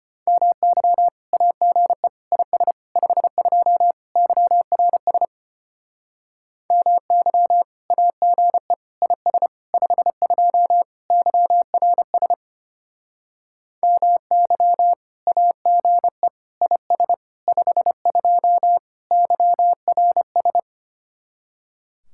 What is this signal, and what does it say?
Morse code, heard as a keyed tone, about 32 wpm, 700 Hz. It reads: MY AGE IS 52 YRS MY AGE IS 52 YRS MY AGE IS 52 YRS